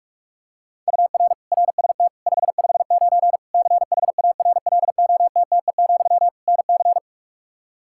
Morse code, heard as a keyed tone, about 46 wpm, 700 Hz. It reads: UR RST 559 CHARLOTTE, NC
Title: Practice QSO - Exchange 1 - Beginning - 45wpm